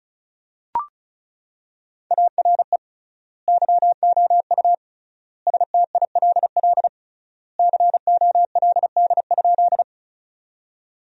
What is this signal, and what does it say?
Morse code, heard as a keyed tone, about 34 wpm, 700 Hz. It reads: ARE YOU STILL COLD?
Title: 3x Word-Spacing - Sentences from Top 400 Words - 35wpm